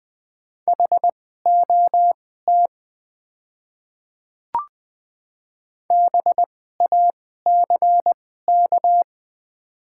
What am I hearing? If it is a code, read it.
Morse code, heard as a keyed tone, about 20 wpm, spaced as for 14 wpm, 700 Hz. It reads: HOT BACK
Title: 2x Character Spacing - Top 300 Words - 20wpm